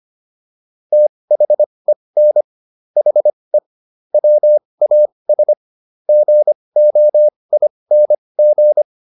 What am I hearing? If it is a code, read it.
Morse code, heard as a keyed tone, about 25 wpm, spaced as for 20 wpm, 600 Hz. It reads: THEN HE WAS GOING